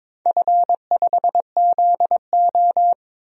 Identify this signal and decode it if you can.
Morse code, heard as a keyed tone, about 22 wpm, 700 Hz. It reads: F5ZO
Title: DXCC-100 Call Signs - 4 Characters - 22wpm